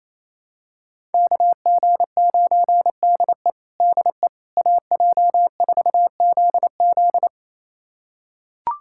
Morse code, heard as a keyed tone, about 28 wpm, 700 Hz. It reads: KG9DE DE AJ4ZZ
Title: USA Callsign DE Callsign - 5 Characters - 28wpm